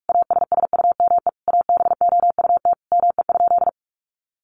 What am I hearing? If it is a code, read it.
Morse code, heard as a keyed tone, about 44 wpm, 700 Hz. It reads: ASSUME ABOUT ME?
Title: Sentences from Top 1500 Words 45wpm